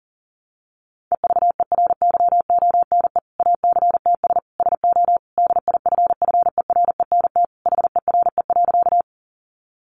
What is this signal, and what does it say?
Morse code, heard as a keyed tone, about 40 wpm, 700 Hz. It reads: EVERYONE ACTS SO DIFFERENT HERE.
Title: Easy Phrases 40wpm